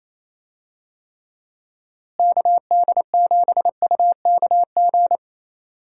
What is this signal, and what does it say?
Morse code, heard as a keyed tone, about 28 wpm, 700 Hz. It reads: KD7UKG
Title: USA Call Signs - 6 Characters - Repeated 2x - 28wpm